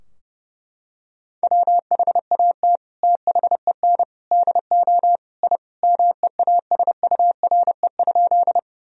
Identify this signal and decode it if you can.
Morse code, heard as a keyed tone, about 30 wpm, 700 Hz. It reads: WHAT THEN DO I MEASURE?
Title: Sentences from Top 1000 Words - Review Part 2 - 30wpm